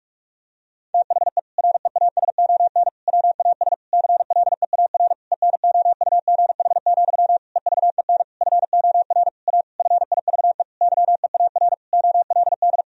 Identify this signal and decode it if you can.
Morse code, heard as a keyed tone, about 46 wpm, 700 Hz. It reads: THE REASON WAS CLEAR ENOUGH, EVEN FOR A FIVE YEAR OLD
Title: Sentences from Top 1000 Words - Review Part 2 - 45wpm